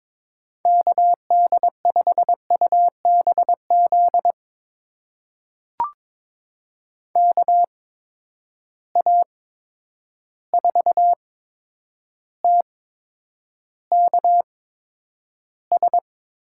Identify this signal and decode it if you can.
Morse code, heard as a keyed tone, about 22 wpm, spaced as for 9 wpm, 700 Hz. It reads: KD5UBZ KA4TKS
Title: USA Call Signs - 6 Characters - 8x Character Spacing - 22wpm